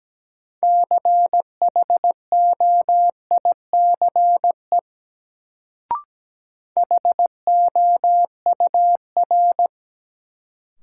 Morse code, heard as a keyed tone, about 17 wpm, 700 Hz. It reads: CHOICE HOUR
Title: Sets of 5 Words - New Word Drill - 17wpm